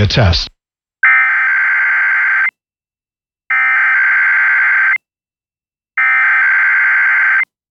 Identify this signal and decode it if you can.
Emergency Alert System (SAME) tones (header bursts), heard as harsh buzzing bursts, with speech. It is heard from 1.0 to 6.6 s.